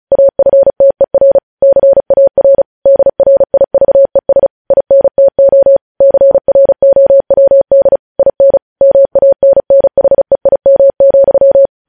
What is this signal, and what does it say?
Morse code, heard as a keyed tone, about 36 wpm, 550 Hz. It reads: AFTER CAR DRIVES INTO CROWD IN MANNHEIM,
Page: News Headlines in Morse Code at 35 WPM